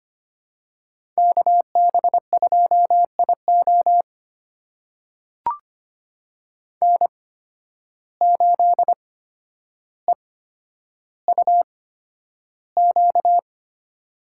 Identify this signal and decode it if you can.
Morse code, heard as a keyed tone, about 25 wpm, spaced as for 11 wpm, 700 Hz. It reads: KB2IO N8EUQ
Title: USA Call Signs - 5 Characters - 8x Character Spacing - 25wpm